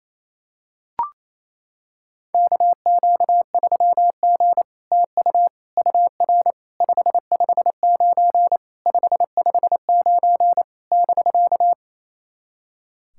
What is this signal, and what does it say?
Morse code, heard as a keyed tone, about 27 wpm, 700 Hz. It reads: KQ3G TU UR 559 559 <BK>
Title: SOTA QSO Practice 28wpm